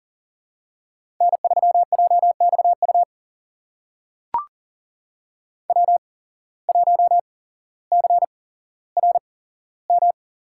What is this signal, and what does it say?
Morse code, heard as a keyed tone, about 40 wpm, spaced as for 17 wpm, 700 Hz. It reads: N3JXU W1CRM
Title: USA Call Signs - 5 Characters - 8x Character Spacing - 40wpm